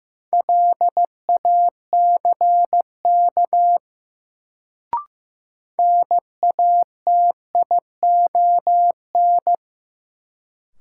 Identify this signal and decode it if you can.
Morse code, heard as a keyed tone, about 15 wpm, 700 Hz. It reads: LACK NATION
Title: Sets of 3 Words - New Word Drill - 15wpm